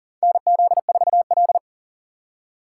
Morse code, heard as a keyed tone, about 39 wpm, 700 Hz. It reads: NZ4L
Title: USA Call Signs - 4 Characters - 6x Character Spacing - 40wpm